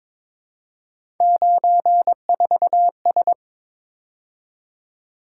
Morse code, heard as a keyed tone, about 26 wpm, 700 Hz. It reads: 94S
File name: Speed-Racing - Three Letter-Number Combinations - 22wpm